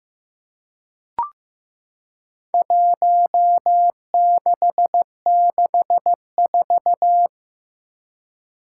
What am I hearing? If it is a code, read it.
Morse code, heard as a keyed tone, about 15 wpm, 700 Hz. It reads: 1664
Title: Four Challenging Letter-Number Combinations 15wpm